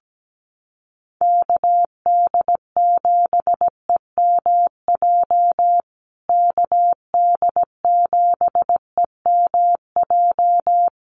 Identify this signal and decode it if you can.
Morse code, heard as a keyed tone, about 17 wpm, 700 Hz. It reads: KD7EMJ KD7EMJ
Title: USA Call Signs Repeated 3x 17wpm